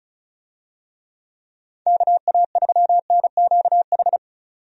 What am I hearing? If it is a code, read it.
Morse code, heard as a keyed tone, about 36 wpm, 700 Hz. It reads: KA3NQH
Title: USA Call Signs - 6 Characters - 35wpm